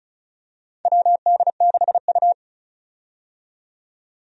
Morse code, heard as a keyed tone, about 36 wpm, 700 Hz. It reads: WD6U
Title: USA Call Signs - 4 Characters - 35wpm